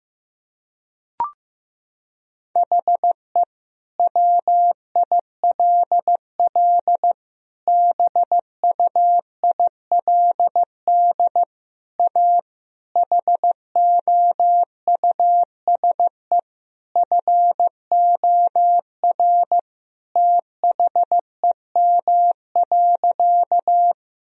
Morse code, heard as a keyed tone, about 15 wpm, 700 Hz. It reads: HE WILL BUILD A HOUSE FOR THEM.
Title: Sentences from Top 1000 Words - Review Part 1 - 15wpm